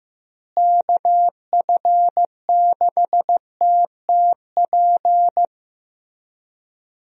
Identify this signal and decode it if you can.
Morse code, heard as a keyed tone, about 15 wpm, 700 Hz. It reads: KF6TTP
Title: USA Call Signs - 6 Characters - Repeated 3x - 15wpm